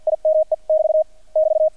Morse code, heard as a keyed tone, about 50 wpm, spaced as for 44 wpm, 630 Hz, with radio noise. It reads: IME <BT> <BT>
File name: archived / CRSnet